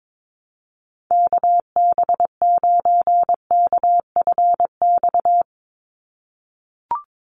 Morse code, heard as a keyed tone, about 22 wpm, 700 Hz. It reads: KB9KFX
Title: USA Call Signs 22wpm